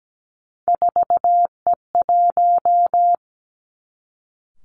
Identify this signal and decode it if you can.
Morse code, heard as a keyed tone, about 17 wpm, 700 Hz. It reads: 4E1
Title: Three Challenging Letter-Number Combinations 17wpm